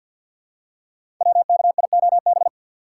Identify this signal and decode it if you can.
Morse code, heard as a keyed tone, about 51 wpm, 700 Hz. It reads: WKIOB
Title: Code Groups - 5 Letters - 50wpm